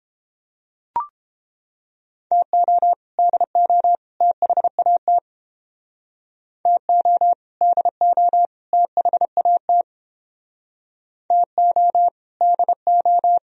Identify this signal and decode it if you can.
Morse code, heard as a keyed tone, about 29 wpm, 700 Hz. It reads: TO DO THAT TO DO THAT TO DO